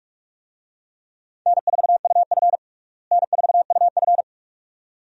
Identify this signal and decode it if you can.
Morse code, heard as a keyed tone, about 43 wpm, 700 Hz. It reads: N4UF N4UF
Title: USA Call Signs - 4 Characters - Repeated 2x - 45wpm